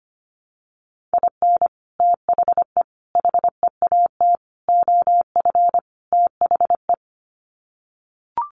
Morse code, heard as a keyed tone, about 25 wpm, 700 Hz. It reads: IN THE HEAT OF THE
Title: Sets of 5 Words Encore2 25wpm